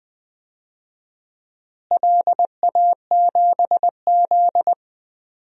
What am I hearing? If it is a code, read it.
Morse code, heard as a keyed tone, about 20 wpm, 700 Hz. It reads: LA7Z
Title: DXCC-100 Call Signs - 4 Characters - Repeated 3x - 20wpm